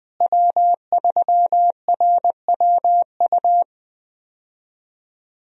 Morse code, heard as a keyed tone, about 21 wpm, 700 Hz. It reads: W3RWU
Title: USA Call Signs - 5 Characters - Repeated 3x - 20wpm